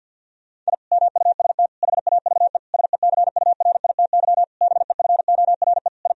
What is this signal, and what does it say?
Morse code, heard as a keyed tone, about 50 wpm, 700 Hz. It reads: I MUST HAVE SECURITY BEFORE I
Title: Sentences from Top 1000 Words - Review Part 4 - 50wpm